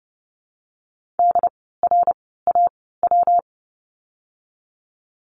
Morse code, heard as a keyed tone, about 30 wpm, spaced as for 17 wpm, 700 Hz. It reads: DRAW
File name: course / 30:10wpm / 115 Lesson 19f - Letter w - Words 30wpm